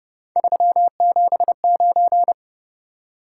Morse code, heard as a keyed tone, about 31 wpm, 700 Hz. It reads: 379
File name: Speed-Racing - Three Letter-Number Combinations - 25wpm